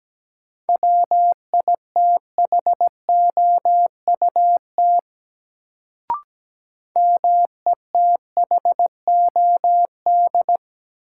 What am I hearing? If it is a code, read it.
Morse code, heard as a keyed tone, about 17 wpm, 700 Hz. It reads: WITHOUT METHOD
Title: Sentences from Top 600 Words - New Word Drill - 17wpm